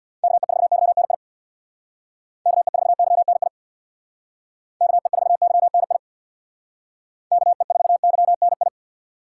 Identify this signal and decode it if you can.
Morse code, heard as a keyed tone, about 66 wpm, 700 Hz. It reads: KE4YNI KE4YNI KE4YNI KE4YNI